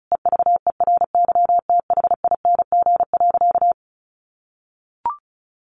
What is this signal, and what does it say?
Morse code, heard as a keyed tone, about 34 wpm, 700 Hz. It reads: EVERYTHING.
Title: Sentences from Top 900 Words 35wpm